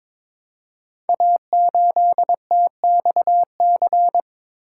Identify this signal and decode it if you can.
Morse code, heard as a keyed tone, about 22 wpm, 700 Hz. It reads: A8TXC
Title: Code Groups - 5 Letter:Numbers - 22wpm